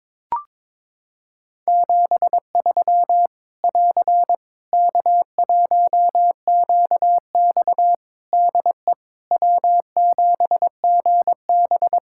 Morse code, heard as a keyed tone, about 22 wpm, 700 Hz. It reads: 73 <AR> K1QX DE W7GB
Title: K1USN SST QSO Practice 22wpm